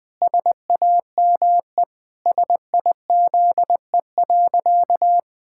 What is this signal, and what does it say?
Morse code, heard as a keyed tone, about 21 wpm, 700 Hz. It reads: SAME SIZE.
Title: Pangram Challenge v2 - 20wpm